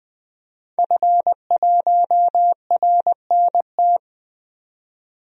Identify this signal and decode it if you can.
Morse code, heard as a keyed tone, about 20 wpm, 700 Hz. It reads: F1RNT
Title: DXCC-100 Call Signs - 5 Characters - 20wpm